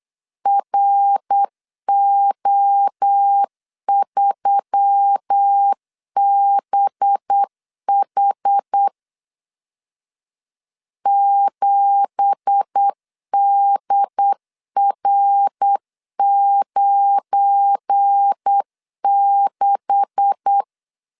Morse code, heard as a keyed tone, about 8 wpm, 800 Hz. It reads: RO3BH 7DR96
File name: 8wpm